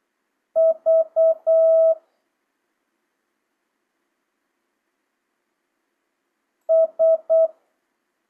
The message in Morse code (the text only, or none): V S